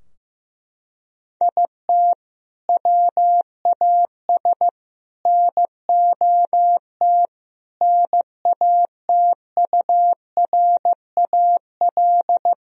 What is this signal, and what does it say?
Morse code, heard as a keyed tone, about 15 wpm, 700 Hz. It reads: IT WAS NOT NATURAL
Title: Sentences from Top 1000 Words - Review Part 4 - 15wpm